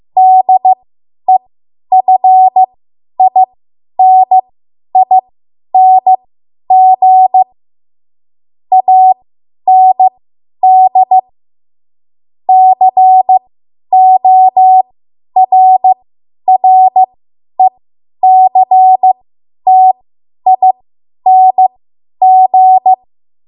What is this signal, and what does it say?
Morse code, heard as a keyed tone, about 15 wpm, spaced as for 10 wpm, 750 Hz. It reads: DEFINING AND CORRECTING
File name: Morse / 10WPM